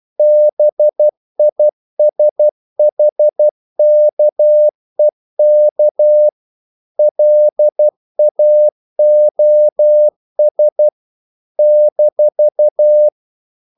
Morse code, heard as a keyed tone, about 12 wpm, 600 Hz. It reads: BISHKEK LAOS -
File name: CW / Practice / NationalCapitals12